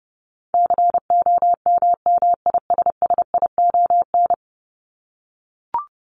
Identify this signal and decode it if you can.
Morse code, heard as a keyed tone, about 29 wpm, 700 Hz. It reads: COMMISSION
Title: Top 1251-1500 Words 30wpm